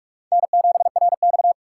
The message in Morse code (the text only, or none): N7RX